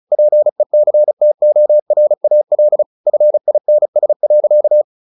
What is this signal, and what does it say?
Morse code, heard as a keyed tone, about 34 wpm, 600 Hz. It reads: PECTORAL FINS.
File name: the old man and the sea 2 35wpm